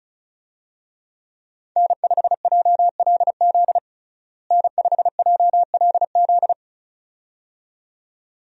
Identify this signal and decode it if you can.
Morse code, heard as a keyed tone, about 34 wpm, 700 Hz. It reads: N5JLZ N5JLZ
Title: USA Call Signs - 5 Characters - Repeated 2x - 35wpm